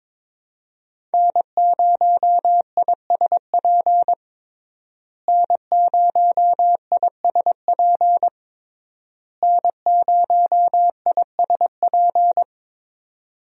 Morse code, heard as a keyed tone, about 23 wpm, 700 Hz. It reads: N0ISP N0ISP N0ISP
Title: USA Call Signs - 5 Characters - Repeated 3x - 22wpm